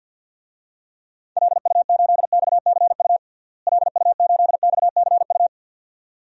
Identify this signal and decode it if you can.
Morse code, heard as a keyed tone, about 46 wpm, 700 Hz. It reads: LU8XCU LU8XCU
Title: DXCC-100 Call Signs - 6 Characters - Repeated 2x - 50wpm